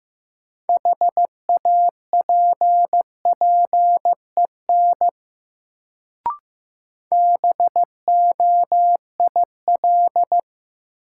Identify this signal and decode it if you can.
Morse code, heard as a keyed tone, about 15 wpm, 700 Hz. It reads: HAPPEN BOIL
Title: Sets of 5 Words Encore - New Word Drill - 15wpm